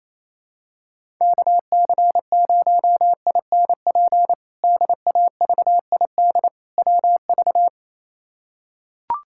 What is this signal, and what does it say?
Morse code, heard as a keyed tone, about 28 wpm, 700 Hz. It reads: KC0INP DAVID WV
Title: K1USN SST QSO Practice - 28wpm